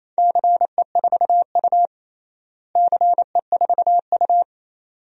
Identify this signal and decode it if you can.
Morse code, heard as a keyed tone, about 28 wpm, 700 Hz. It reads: CE4U CE4U
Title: DXCC-100 Call Signs - 4 Characters - Repeated 2x - 28wpm